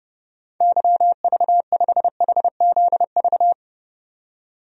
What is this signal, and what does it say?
Morse code, heard as a keyed tone, about 31 wpm, 700 Hz. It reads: YV5HZV
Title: DXCC-100 Call Signs - 6 Characters - 30wpm